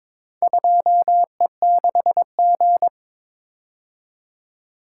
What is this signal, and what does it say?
Morse code, heard as a keyed tone, about 22 wpm, 700 Hz. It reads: 2E6G
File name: practice / 22wpm / Four Letter-Number Combinations - 22wpm